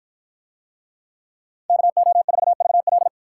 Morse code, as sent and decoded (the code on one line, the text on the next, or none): -.- --- ....- ...- .-..
KO4VL